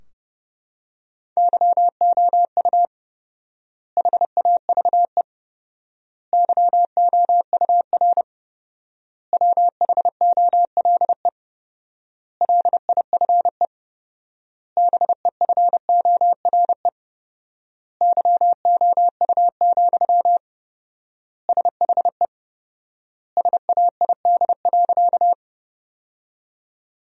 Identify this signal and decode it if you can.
Morse code, heard as a keyed tone, about 32 wpm, 700 Hz. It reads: YOU HAVE YOUR WHOLE LIFE BEFORE YOU, SHE SAID.
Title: 4x Word-Spacing - Sentences from Top 400 Words - 30wpm